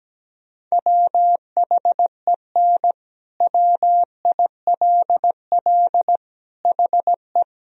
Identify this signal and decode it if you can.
Morse code, heard as a keyed tone, about 17 wpm, 700 Hz. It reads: WHEN WILL HE
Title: Sentences from Top 1000 Words - Review Part 1 - 17wpm